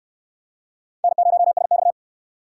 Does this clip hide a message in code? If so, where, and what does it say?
Yes: Morse code from 1.0 to 1.9 s, heard as a keyed tone, about 69 wpm, 700 Hz. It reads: N0SX